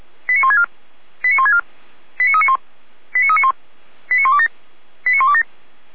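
Five-tone selective call (ZVEI), heard as quick runs of five tones, tones of about 80 ms. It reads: 89164 89164 89391 89391 89127 89127
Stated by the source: ZVEI five-tone selective call received on an SDR.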